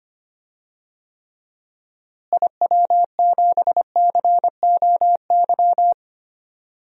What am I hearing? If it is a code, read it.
Morse code, heard as a keyed tone, about 25 wpm, 700 Hz. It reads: IW7COY
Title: DXCC-100 Call Signs - 6 Characters - Repeated 2x - 25wpm